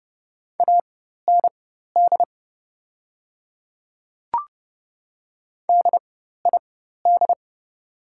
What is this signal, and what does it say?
Morse code, heard as a keyed tone, about 30 wpm, spaced as for 14 wpm, 700 Hz. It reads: AND DID
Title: 4x Character Spacing - Top 100 Words - 30wpm